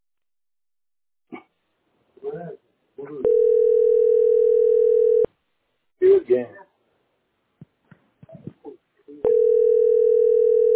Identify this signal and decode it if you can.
Ringback tone (North America), 440 + 480 Hz, with speech.
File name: dbd89ac4-dac0-4103-92de